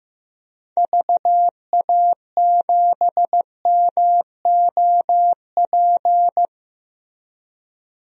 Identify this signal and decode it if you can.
Morse code, heard as a keyed tone, about 15 wpm, 700 Hz. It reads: VA7MOP
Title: DXCC-100 Call Signs - 6 Characters - Repeated 3x - 15wpm